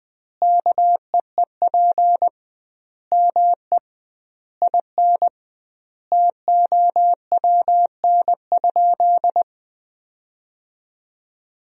Morse code, heard as a keyed tone, about 20 wpm, 700 Hz. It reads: KEEP ME IN TOWN?
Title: 2x Word-Spacing - Sentences from Top 500 Words - 20wpm